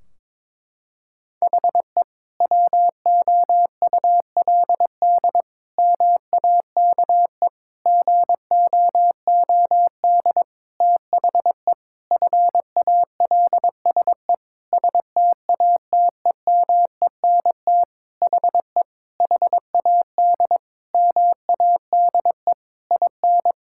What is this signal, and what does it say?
Morse code, heard as a keyed tone, about 22 wpm, 700 Hz. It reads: HE WOULD MAKE GOOD THE FALSE STATEMENT HE HAD MADE IN
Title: Sentences from Top 1600 Words 22wpm